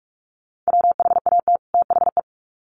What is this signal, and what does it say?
Morse code, heard as a keyed tone, about 44 wpm, 700 Hz. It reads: WHAT THE
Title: Sets of 2 Words 45wpm